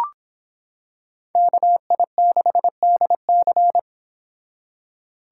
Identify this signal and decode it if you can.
Morse code, heard as a keyed tone, about 26 wpm, 700 Hz. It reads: KI6DC